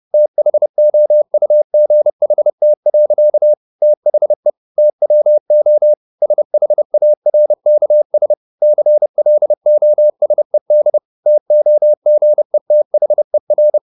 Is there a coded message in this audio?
Yes: Morse code from 0.1 to 13.8 s, heard as a keyed tone, about 30 wpm, 600 Hz. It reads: THOUGHT. THE TWO SHARKS CLOSED TOGETHER